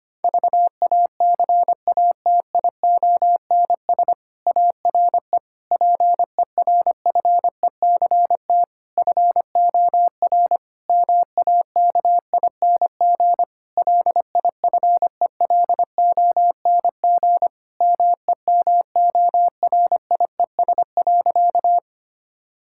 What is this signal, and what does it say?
Morse code, heard as a keyed tone, about 25 wpm, 700 Hz. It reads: VACATIONS ARE PERFECT FOR MAKING LIFELONG MEMORIES.
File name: practice / 25wpm / Difficult Sentences with Hints - 25wpm